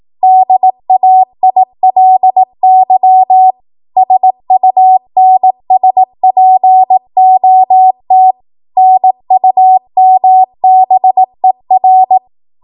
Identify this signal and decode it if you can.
Morse code, heard as a keyed tone, about 18 wpm, 750 Hz. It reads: DAILY SUNSPOT NUMBER
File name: Archive / 18 WPM / 18WPM